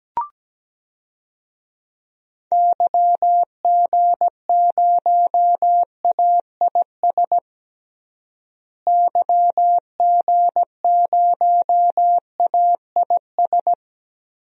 Morse code, heard as a keyed tone, about 17 wpm, 700 Hz. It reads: YG0AIS YG0AIS
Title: DXCC-100 Call Signs - 6 Characters - Repeated 3x - 17wpm